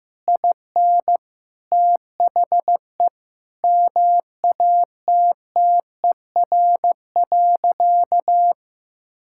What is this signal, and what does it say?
Morse code, heard as a keyed tone, about 15 wpm, 700 Hz. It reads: IN THE MATTER.
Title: Sentences from Top 1000 Words - Review Part 5 - 15wpm